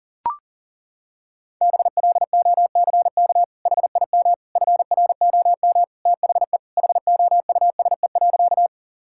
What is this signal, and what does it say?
Morse code, heard as a keyed tone, about 40 wpm, 700 Hz. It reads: BLOCK HIM FROM THE HOUSE.
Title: Sentences from Top 1600 Words 40wpm